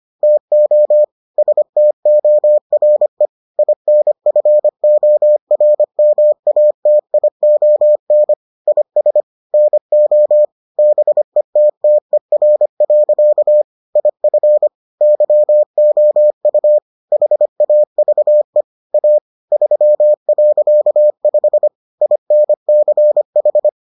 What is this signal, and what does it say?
Morse code, heard as a keyed tone, about 25 wpm, 600 Hz. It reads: TO STORE INFORMATION IS NO BETTER. IF YOU HAVE A 3.5 INCH